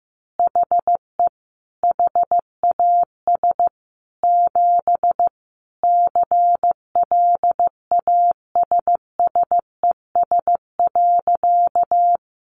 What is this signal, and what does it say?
Morse code, heard as a keyed tone, about 15 wpm, 700 Hz. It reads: HE HAS 7 CLASSES.
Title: Additional Easy Phrases 15wpm